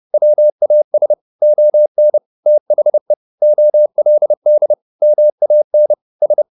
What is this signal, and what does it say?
Morse code, heard as a keyed tone, about 30 wpm, 600 Hz. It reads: WAS ON THE OLD MAN S
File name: the old man and the sea / the old man and the sea 1 30wpm